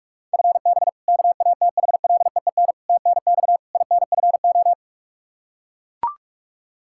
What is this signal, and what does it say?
Morse code, heard as a keyed tone, about 41 wpm, 700 Hz. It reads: FB KATHLEEN TNX INFO
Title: Practice QSO - Exchange 3 - More Info - 45wpm